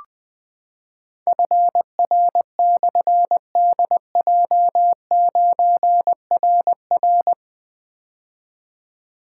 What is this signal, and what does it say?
Morse code, heard as a keyed tone, about 20 wpm, 700 Hz. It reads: FR/DJ9RR
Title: Difficult Call Signs 20wpm